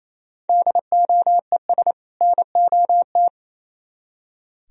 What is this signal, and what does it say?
Morse code, heard as a keyed tone, about 35 wpm, 700 Hz. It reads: DOES NOT
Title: Speed-Racing - Sets of 2 Words 28wpm